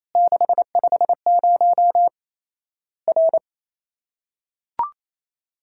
Morse code, heard as a keyed tone, about 27 wpm, 700 Hz. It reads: 650 R
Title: Worked All Europe DX Contest - QTC Exchange - 28wpm